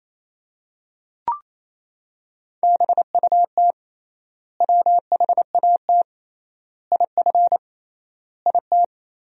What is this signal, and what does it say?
Morse code, heard as a keyed tone, about 28 wpm, 700 Hz. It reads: BUT WHAT IF IT
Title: 3x Word-Spacing - Sentences from Top 100 Words - 28wpm